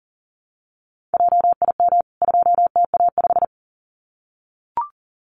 Morse code, heard as a keyed tone, about 38 wpm, 700 Hz. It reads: JIM 2TA5
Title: CWT Exchange 40wpm